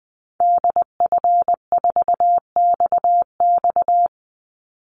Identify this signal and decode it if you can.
Morse code, heard as a keyed tone, about 20 wpm, 700 Hz. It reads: DF4XX
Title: Call Signs Repeated 3x - 20wpm